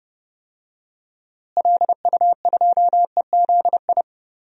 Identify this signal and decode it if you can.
Morse code, heard as a keyed tone, about 30 wpm, 700 Hz. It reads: LU2EZI
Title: DXCC-100 Call Signs - 6 Characters - Repeated 2x - 30wpm